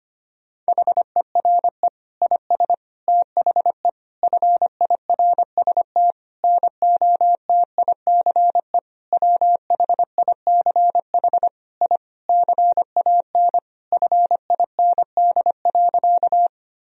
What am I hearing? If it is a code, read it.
Morse code, heard as a keyed tone, about 25 wpm, 700 Hz. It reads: HERE IS THE FIRST NOTICE WHICH I CAN FIND.
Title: Sentences from Top 1000 Words - Review Part 2 - 25wpm